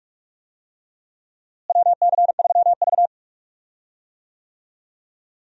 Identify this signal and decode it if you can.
Morse code, heard as a keyed tone, about 47 wpm, 700 Hz. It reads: WC3V
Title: USA Call Signs - 4 Characters - Repeated 3x - 45wpm